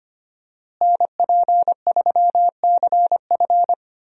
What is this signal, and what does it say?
Morse code, heard as a keyed tone, about 25 wpm, 700 Hz. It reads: NP3CF
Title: USA Call Signs - 5 Characters - Repeated 2x - 25wpm